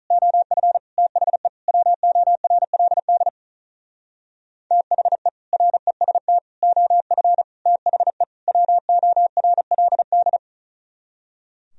Morse code, heard as a keyed tone, about 43 wpm, 700 Hz. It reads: OF THE WORLD THE REST OF THE WORLD
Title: Speed-Racing - Sets of 5 Words 35wpm